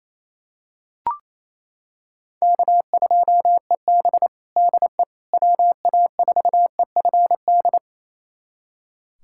Morse code, heard as a keyed tone, about 28 wpm, 700 Hz. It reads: K2EB DE WA4EFD
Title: USA Callsign DE Callsign 28wpm